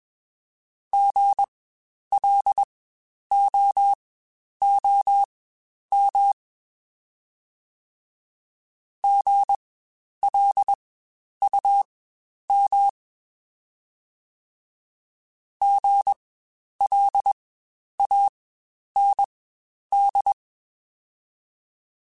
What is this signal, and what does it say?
Morse code, heard as a keyed tone, about 21 wpm, spaced as for 10 wpm, 800 Hz. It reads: GLOOM GLUM GLAND
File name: Spaced Alphas